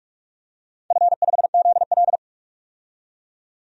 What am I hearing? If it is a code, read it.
Morse code, heard as a keyed tone, about 46 wpm, 700 Hz. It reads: F5ZL
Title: DXCC-100 Call Signs - 4 Characters - 45wpm